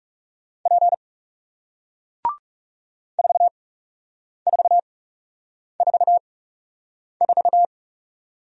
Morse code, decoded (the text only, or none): P 4444